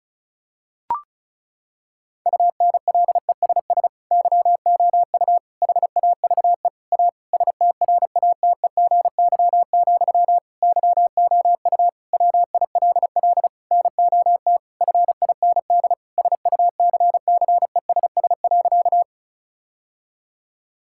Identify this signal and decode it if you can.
Morse code, heard as a keyed tone, about 36 wpm, 700 Hz. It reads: UNLESS YOU HAVE A STRATEGY, YOU WILL NOT FIND SUCCESS.
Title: Sentences from Top 1000 Words - Review Part 5 - 35wpm